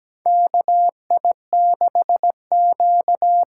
Morse code, heard as a keyed tone, about 17 wpm, 700 Hz. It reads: KI6Q
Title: USA Call Signs - 6 Characters - 8x Character Spacing - 17wpm